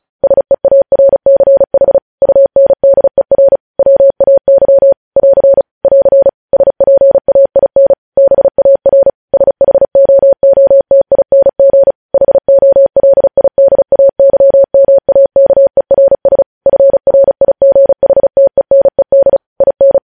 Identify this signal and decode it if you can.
Morse code, heard as a keyed tone, about 36 wpm, 550 Hz. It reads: SEARCH UNDER WAY <AR> <AR> SPAIN BAR SHOOTING HOLIDAYMAKERS FRIGHTENED IN
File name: podcast / News headlines-1x-35x35